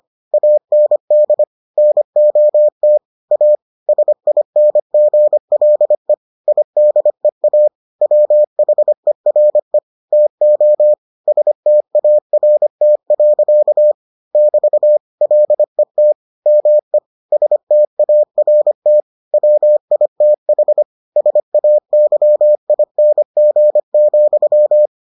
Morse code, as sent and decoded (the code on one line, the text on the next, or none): .- -. -.. / -. --- - / .- / ... .. -. --. .-.. / .. -.. .- / .-- .... .-. / - --- / ... - .- .-. - .-.-.- / -...- / .-.. - / -- . / ... - .- .-. - / .-- .. - .... / ... .- -.-- .. -. --. --..--
AND NOT A SINGLE IDEA WHERE TO START. <BT> LET ME START WITH SAYING,